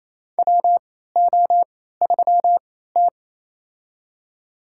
Morse code, heard as a keyed tone, about 28 wpm, spaced as for 16 wpm, 700 Hz. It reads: WO3T
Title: USA Call Signs - 4 Characters - 3x Character Spacing - 28wpm